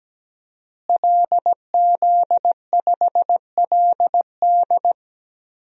Morse code, heard as a keyed tone, about 17 wpm, 700 Hz. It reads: LZ5LD
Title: DXCC-100 Call Signs - 5 Characters - Repeated 2x - 17wpm